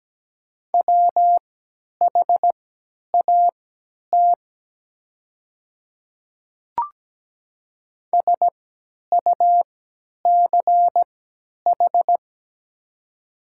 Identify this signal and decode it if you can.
Morse code, heard as a keyed tone, about 17 wpm, spaced as for 10 wpm, 700 Hz. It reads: WHAT SUCH
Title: 3x Character Spacing - Top 200 Words - 17wpm